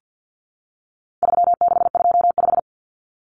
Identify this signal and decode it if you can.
Morse code, heard as a keyed tone, about 50 wpm, 700 Hz. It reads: 3625
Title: Common Cut Numbers - 4-Digit 50wpm